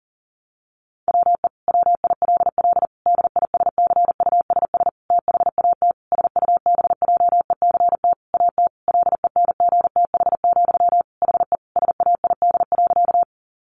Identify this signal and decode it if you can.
Morse code, heard as a keyed tone, about 40 wpm, 700 Hz. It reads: WE WILL DISCUSS THAT SUBJECT AT LENGTH, HE SAID.
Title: Sentences from Top 1000 Words 40wpm